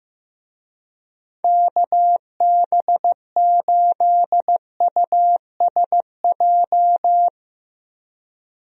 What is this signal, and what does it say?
Morse code, heard as a keyed tone, about 15 wpm, 700 Hz. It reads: KB8USJ
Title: USA Call Signs - 6 Characters - Repeated 3x - 15wpm